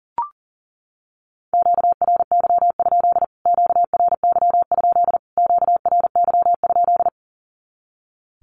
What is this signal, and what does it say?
Morse code, heard as a keyed tone, about 40 wpm, 700 Hz. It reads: QRY? QRY? QRY?
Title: Q-Codes Repeated 3x 40wpm